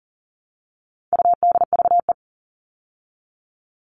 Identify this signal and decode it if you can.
Morse code, heard as a keyed tone, about 40 wpm, 700 Hz. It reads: UDVE
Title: All Cut Numbers - 4-Digit 40wpm